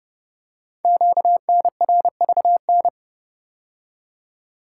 Morse code, heard as a keyed tone, about 30 wpm, 700 Hz. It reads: QNRVN
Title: Code Groups - 5 Letters - 30wpm